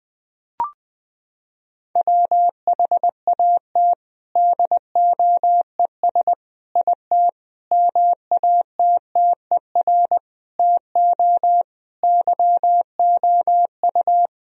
Morse code, heard as a keyed tone, about 20 wpm, 700 Hz. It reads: WHAT DOES IT MATTER TO YOU
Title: Sentences from Top 1000 Words - Review Part 3 - 20wpm